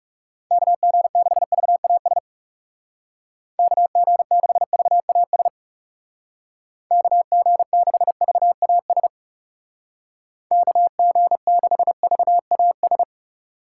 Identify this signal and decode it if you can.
Morse code, heard as a keyed tone, about 38 wpm, 700 Hz. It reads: KG6VAS KG6VAS KG6VAS KG6VAS